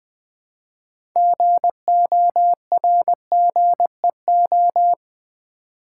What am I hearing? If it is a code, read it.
Morse code, heard as a keyed tone, about 20 wpm, 700 Hz. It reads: GORGEO
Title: Build A Word - 2 Letters At A Time - 8-Letter Words - 20wpm